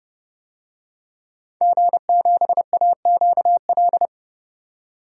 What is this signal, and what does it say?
Morse code, heard as a keyed tone, about 30 wpm, 700 Hz. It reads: G7AQL
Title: DXCC-100 Call Signs - 5 Characters - 30wpm